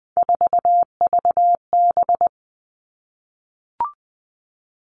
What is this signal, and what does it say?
Morse code, heard as a keyed tone, about 20 wpm, 700 Hz. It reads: 4VB
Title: Three Challenging Letter-Number Combinations 20wpm